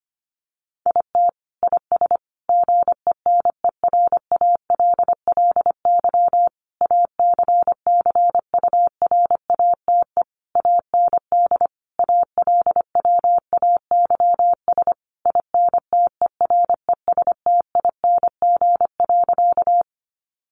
Morse code, heard as a keyed tone, about 25 wpm, 700 Hz. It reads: IT IS GENERALLY ACCURATE AND ALWAYS INTERESTING.
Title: Sentences from Top 1200 Words 25wpm